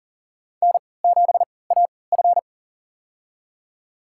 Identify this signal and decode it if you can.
Morse code, heard as a keyed tone, about 40 wpm, spaced as for 23 wpm, 700 Hz. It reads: N7AF
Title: USA Call Signs - 3x Character Spacing - 40wpm